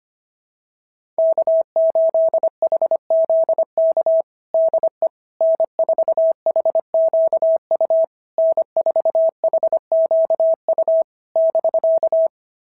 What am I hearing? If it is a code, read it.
Morse code, heard as a keyed tone, about 25 wpm, 660 Hz. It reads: K8HZK DE N4HQU N4HQU <BK>
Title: SOTA QSO Practice 25wpm